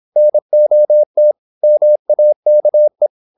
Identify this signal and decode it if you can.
Morse code, heard as a keyed tone, about 26 wpm, 600 Hz. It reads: NOT MAKE
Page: Never 26wpm | CW med Gnister